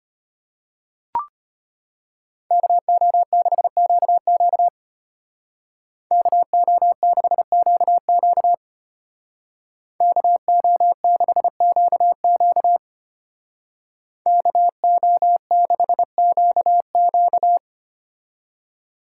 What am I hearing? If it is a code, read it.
Morse code, heard as a keyed tone, about 31 wpm, 700 Hz. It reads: KO6QQ KO6QQ KO6QQ KO6QQ
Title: Speed Racing - USA 2x2 Call Signs 25wpm